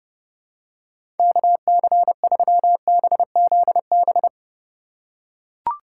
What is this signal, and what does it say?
Morse code, heard as a keyed tone, about 30 wpm, 700 Hz. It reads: KC3BZB
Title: USA Call Signs - 6 Characters - 3x Character Spacing - 30wpm